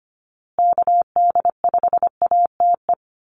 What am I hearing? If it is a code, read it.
Morse code, heard as a keyed tone, about 25 wpm, 700 Hz. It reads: KD5ATE
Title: USA Call Signs Repeated 2x 25wpm